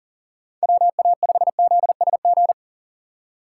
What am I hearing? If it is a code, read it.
Morse code, heard as a keyed tone, about 40 wpm, 700 Hz. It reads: WA5ZSG
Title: USA Call Signs - 6 Characters - 2x Character Spacing - 40wpm